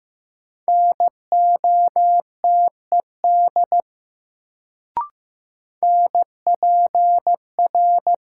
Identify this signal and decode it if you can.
Morse code, heard as a keyed tone, about 15 wpm, 700 Hz. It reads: NOTED NPR
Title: Sets of 5 Words - New Word Drill - 15wpm